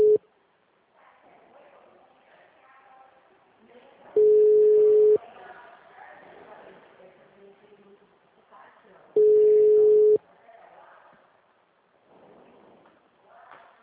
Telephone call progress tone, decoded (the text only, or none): ringback tone